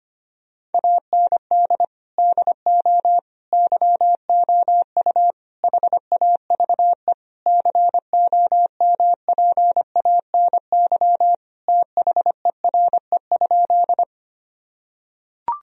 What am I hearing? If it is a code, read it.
Morse code, heard as a keyed tone, about 25 wpm, 700 Hz. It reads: AND DO YOU HAVE COMPANY THERE?
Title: Sentences from Top 1000 Words - Review Part 3 - 25wpm